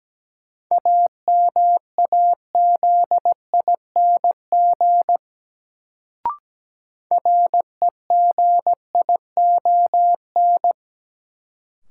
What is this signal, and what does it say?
Morse code, heard as a keyed tone, about 17 wpm, 700 Hz. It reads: AMAZING REGION
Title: Sentences from Top 1500 Words - New Word Drill Alt - 17wpm